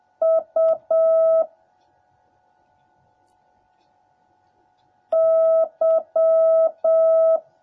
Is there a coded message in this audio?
Yes: Morse code from 0.2 to 7.4 s, heard as a keyed tone, about 7 wpm, 650 Hz. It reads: U Y